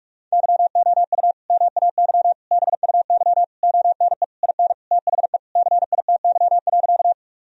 Morse code, heard as a keyed tone, about 41 wpm, 700 Hz. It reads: YOU MAY BUY ONE IN THE CITY.
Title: Sentences from Top 1000 Words - Review Part 4 - 45wpm